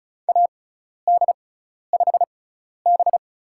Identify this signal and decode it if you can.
Morse code, heard as a keyed tone, about 34 wpm, spaced as for 12 wpm, 700 Hz. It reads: AD5B
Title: USA Call Signs - 4 Characters - 6x Character Spacing - 35wpm